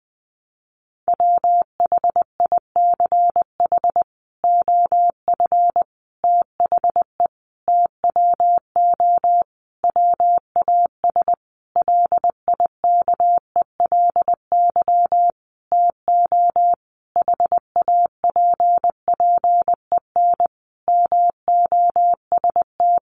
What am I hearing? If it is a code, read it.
Morse code, heard as a keyed tone, about 20 wpm, 700 Hz. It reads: WHICH OF THE TWO WAS LIKELY TO HAPPEN MOST
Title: Sentences from Top 1300 Words 20wpm